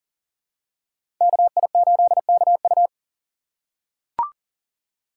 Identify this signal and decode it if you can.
Morse code, heard as a keyed tone, about 40 wpm, 700 Hz. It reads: KI8KU
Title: USA Call Signs - 5 Characters - 2x Character Spacing - 40wpm